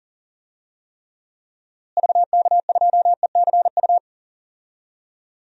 Morse code, heard as a keyed tone, about 39 wpm, 700 Hz. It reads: VK2ECU
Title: DXCC-100 Call Signs - 6 Characters - Repeated 3x - 40wpm